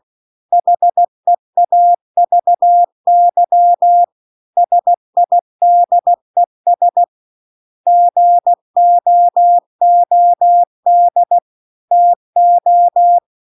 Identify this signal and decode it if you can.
Morse code, heard as a keyed tone, about 16 wpm, 700 Hz. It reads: HEAVY SIDES GOOD TO
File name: CW-The-Big-Twoheaded-River-2-16WPM